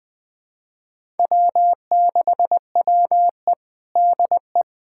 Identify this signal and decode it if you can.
Morse code, heard as a keyed tone, about 21 wpm, 700 Hz. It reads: W6WE DE
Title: USA Callsign DE Callsign - 4 Characters - 20wpm